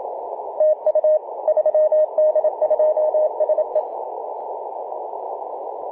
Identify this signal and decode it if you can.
Morse code, heard as a keyed tone, about 27 wpm, 620 Hz, with radio noise. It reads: TU 3D2SE
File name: 3d2se-18-080mhz-cw